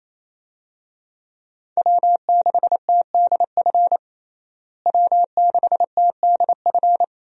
Morse code, heard as a keyed tone, about 27 wpm, 700 Hz. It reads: W6TDF W6TDF
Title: USA Call Signs - 5 Characters - Repeated 3x - 28wpm